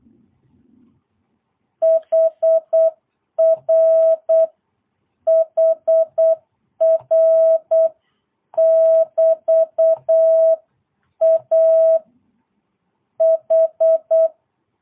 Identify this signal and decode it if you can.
Morse code, heard as a keyed tone, about 8 wpm, 660 Hz. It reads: HR HR <BT> A H
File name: jh8cbh / 2022cw